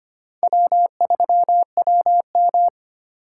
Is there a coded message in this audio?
Yes: Morse code from 0.4 to 2.7 s, heard as a keyed tone, about 32 wpm, 700 Hz. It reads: W3WM